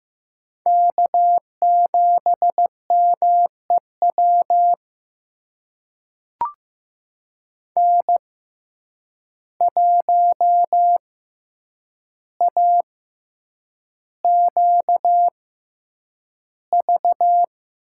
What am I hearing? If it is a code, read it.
Morse code, heard as a keyed tone, about 15 wpm, spaced as for 8 wpm, 700 Hz. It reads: K7MEW N1AQV
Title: USA Call Signs - 5 Characters - 6x Character Spacing - 15wpm